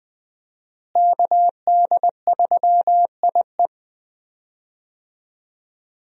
Morse code, heard as a keyed tone, about 20 wpm, 700 Hz. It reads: KD3IE
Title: USA Call Signs - 5 Characters - Repeated 3x - 20wpm